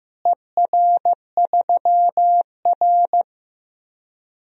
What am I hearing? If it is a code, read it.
Morse code, heard as a keyed tone, about 15 wpm, 700 Hz. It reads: ER3R
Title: DXCC-100 Call Signs - 4 Characters - Repeated 3x - 15wpm